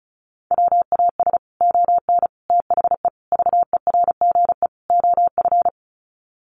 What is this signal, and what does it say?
Morse code, heard as a keyed tone, about 34 wpm, 700 Hz. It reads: WAS ON THE VERGE OF
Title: Sets of 5 Words Encore 35wpm